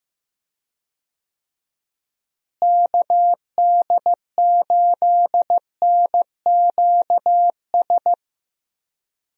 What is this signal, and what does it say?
Morse code, heard as a keyed tone, about 15 wpm, 700 Hz. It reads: KD8NQS
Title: USA Call Signs - 6 Characters - Repeated 3x - 15wpm